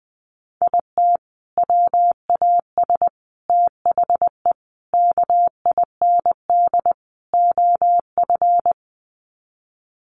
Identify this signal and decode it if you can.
Morse code, heard as a keyed tone, about 20 wpm, 700 Hz. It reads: IT WAS THE KIND OF
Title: Sets of 5 Words Encore2 20wpm